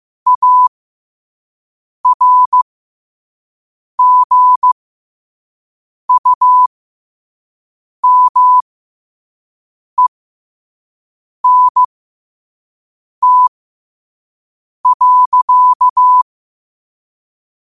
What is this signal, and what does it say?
Morse code, heard as a keyed tone, about 15 wpm, spaced as for 5 wpm, 1000 Hz. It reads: ARGUMENT.
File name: qotdfiles / 5wpm